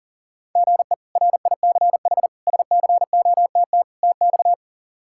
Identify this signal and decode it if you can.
Morse code, heard as a keyed tone, about 40 wpm, 700 Hz. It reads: GE RICH SCOTT TX
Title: K1USN SST QSO Practice - 40wpm